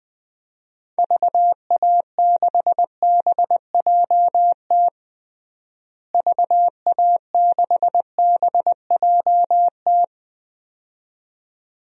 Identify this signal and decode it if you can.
Morse code, heard as a keyed tone, about 20 wpm, 700 Hz. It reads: VA6BJT VA6BJT
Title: DXCC-100 Call Signs - 6 Characters - Repeated 2x - 20wpm